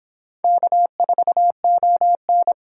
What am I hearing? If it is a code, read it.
Morse code, heard as a keyed tone, about 25 wpm, 700 Hz. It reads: K4ON